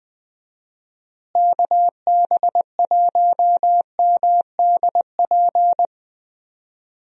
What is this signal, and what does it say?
Morse code, heard as a keyed tone, about 21 wpm, 700 Hz. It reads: KB1MDP
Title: USA Call Signs - 6 Characters - Repeated 2x - 20wpm